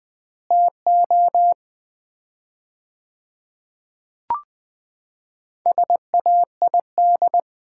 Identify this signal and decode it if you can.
Morse code, heard as a keyed tone, about 21 wpm, 700 Hz. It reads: TO SAID
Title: 4x Word-Spacing - Sets of 2 Words - 20wpm